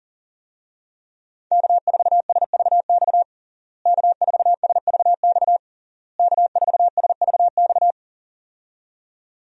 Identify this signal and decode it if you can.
Morse code, heard as a keyed tone, about 40 wpm, 700 Hz. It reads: K4SVX K4SVX K4SVX
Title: USA Call Signs - 5 Characters - Repeated 3x - 40wpm